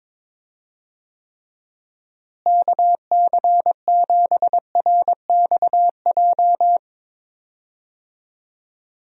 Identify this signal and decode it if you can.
Morse code, heard as a keyed tone, about 22 wpm, 700 Hz. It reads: KC7RXJ
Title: USA Call Signs - 6 Characters - Repeated 3x - 22wpm